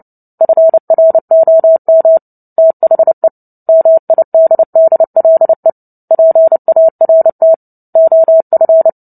Morse code, heard as a keyed tone, about 29 wpm, 650 Hz. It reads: FROM THE MIDDLE PART OF
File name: Book Two / 03-Chapter 3 - 29 WPM